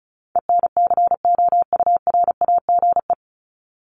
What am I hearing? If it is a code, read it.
Morse code, heard as a keyed tone, about 34 wpm, 700 Hz. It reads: ENCOURAGE
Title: Top 1251-1500 Words 35wpm